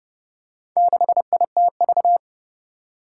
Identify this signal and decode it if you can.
Morse code, heard as a keyed tone, about 31 wpm, 700 Hz. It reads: 6ITV
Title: Four Challenging Letter-Number Combinations 30wpm